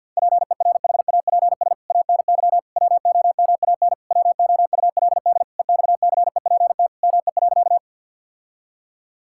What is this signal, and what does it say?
Morse code, heard as a keyed tone, about 52 wpm, 700 Hz. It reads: PERHAPS ANY WOMAN WOULD EXCEPT ME.
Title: Sentences from Top 1600 Words 50wpm